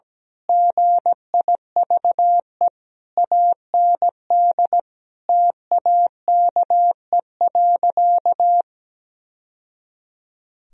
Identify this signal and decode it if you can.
Morse code, heard as a keyed tone, about 17 wpm, 700 Hz. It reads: GIVE AND TAKE.
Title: Sentences from Top 1000 Words - Review Part 4 - 17wpm